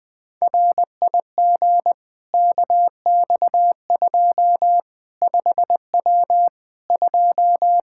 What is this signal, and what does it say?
Morse code, heard as a keyed tone, about 20 wpm, 700 Hz. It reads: RIG KX2 5W 2
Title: Practice QSO - Exchange 2 - Info - 20wpm